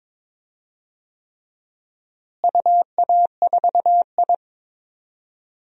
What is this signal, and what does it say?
Morse code, heard as a keyed tone, about 22 wpm, 700 Hz. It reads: UA4I
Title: DXCC-100 Call Signs - 4 Characters - Repeated 3x - 22wpm